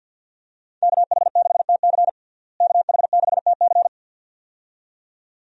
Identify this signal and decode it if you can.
Morse code, heard as a keyed tone, about 50 wpm, 700 Hz. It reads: KH6TC KH6TC
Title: DXCC-100 Call Signs - 5 Characters - Repeated 2x - 50wpm